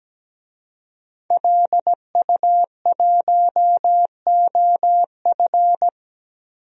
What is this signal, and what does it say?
Morse code, heard as a keyed tone, about 17 wpm, 700 Hz. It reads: LU1OF
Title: DXCC-100 Call Signs - 5 Characters - Repeated 3x - 17wpm